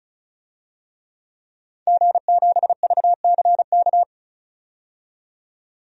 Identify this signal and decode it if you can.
Morse code, heard as a keyed tone, about 35 wpm, 700 Hz. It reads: G7VCK
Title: DXCC-100 Call Signs - 5 Characters - 35wpm